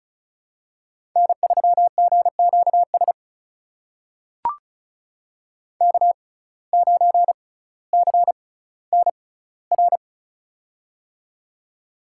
Morse code, heard as a keyed tone, about 36 wpm, spaced as for 18 wpm, 700 Hz. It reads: N3GQS K9CNR
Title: USA Call Signs - 5 Characters - 6x Character Spacing - 35wpm